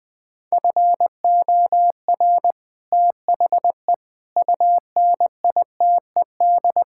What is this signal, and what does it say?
Morse code, heard as a keyed tone, about 21 wpm, 700 Hz. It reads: FOR THE UNITED
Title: Build A Phrase - 4-Word Phrases - 20wpm